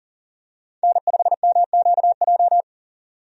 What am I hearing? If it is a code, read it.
Morse code, heard as a keyed tone, about 40 wpm, 700 Hz. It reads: N5MQJ
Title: USA Call Signs - 5 Characters - 40wpm